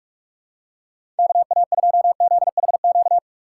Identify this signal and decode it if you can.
Morse code, heard as a keyed tone, about 46 wpm, 700 Hz. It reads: KA2ZHQ
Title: USA Call Signs - 6 Characters - 6x Character Spacing - 45wpm